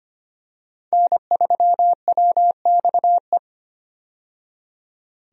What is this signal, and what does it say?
Morse code, heard as a keyed tone, about 25 wpm, 700 Hz. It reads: N3WXE
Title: USA Call Signs - 5 Characters - 25wpm